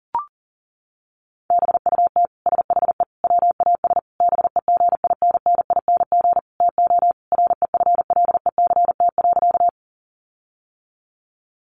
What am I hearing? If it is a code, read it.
Morse code, heard as a keyed tone, about 37 wpm, 700 Hz. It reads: BUT SHE WAS BEGINNING TO REFLECT.
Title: Sentences from Top 1500 Words 40wpm